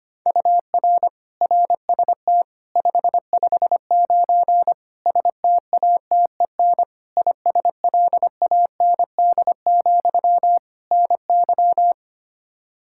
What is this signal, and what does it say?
Morse code, heard as a keyed tone, about 25 wpm, 700 Hz. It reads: UR RST 559 STATEN ISLAND, NY
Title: Practice QSO - Exchange 1 - Beginning - 25wpm